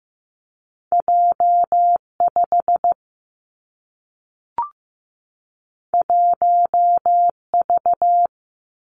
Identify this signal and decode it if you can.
Morse code, heard as a keyed tone, about 15 wpm, 700 Hz. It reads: J 5 1 V
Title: Two Challenging Letter-Number Combinations 15wpm